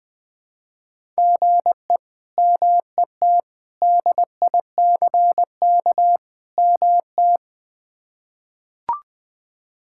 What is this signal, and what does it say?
Morse code, heard as a keyed tone, about 20 wpm, 700 Hz. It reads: GE MET DICK MT
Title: K1USN SST QSO Practice - 20wpm